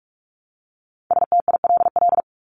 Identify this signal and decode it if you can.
Morse code, heard as a keyed tone, about 44 wpm, 700 Hz. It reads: STILL
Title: Top 500 Words Repeated 45wpm